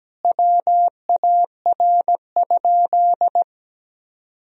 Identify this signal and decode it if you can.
Morse code, heard as a keyed tone, about 17 wpm, 700 Hz. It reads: WAR?
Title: Sentences from Top 1000 Words - Review Part 2 - 17wpm